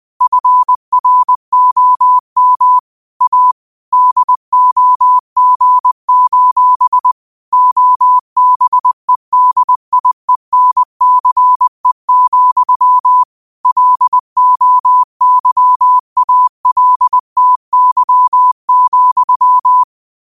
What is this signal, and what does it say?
Morse code, heard as a keyed tone, about 20 wpm, 1000 Hz. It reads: FROM A DOG: OBEDIENCE, LOYALTY,
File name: qotdfiles / 20wpm